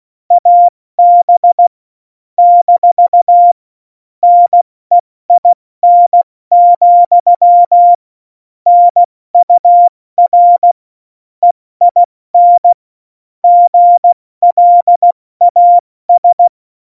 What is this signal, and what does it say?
Morse code, heard as a keyed tone, about 16 wpm, 700 Hz. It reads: AB - NEIN, NUR EIN GLAS
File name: Morsekurs / Witze / Witz 12. Dezember 2017, 70 F80